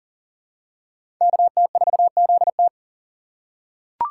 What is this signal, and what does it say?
Morse code, heard as a keyed tone, about 37 wpm, 700 Hz. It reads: KT4ZT
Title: USA Call Signs - 5 Characters - 3x Character Spacing - 40wpm